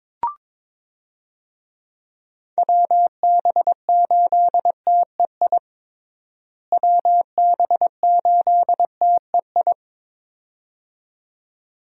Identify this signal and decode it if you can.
Morse code, heard as a keyed tone, about 22 wpm, 700 Hz. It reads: WB8TEI WB8TEI
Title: USA Call Signs - 6 Characters - Repeated 2x - 22wpm